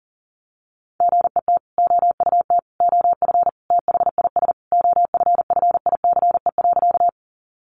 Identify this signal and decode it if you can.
Morse code, heard as a keyed tone, about 37 wpm, 700 Hz. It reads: GET OUT OF THIS OFFICE.
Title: Sentences from Top 1000 Words 40wpm